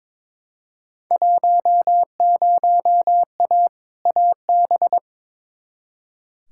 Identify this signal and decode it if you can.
Morse code, heard as a keyed tone, about 21 wpm, 700 Hz. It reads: 10A AB
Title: Field Day Exchanges Encore - 22wpm